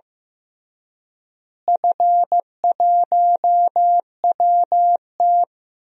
Morse code, heard as a keyed tone, about 15 wpm, 700 Hz. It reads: F1WT
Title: DXCC-100 Call Signs - 4 Characters - Repeated 3x - 15wpm